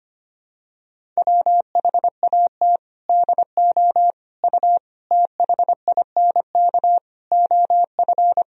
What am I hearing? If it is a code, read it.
Morse code, heard as a keyed tone, about 25 wpm, 700 Hz. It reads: WHAT DO U THINK OF
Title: Sentences from Top 300 Words and Abbreviations 25wpm